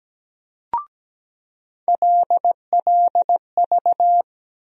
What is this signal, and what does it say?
Morse code, heard as a keyed tone, about 17 wpm, 700 Hz. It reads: LLV
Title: USA Call Sign Postfixes - 3 Letters 17wpm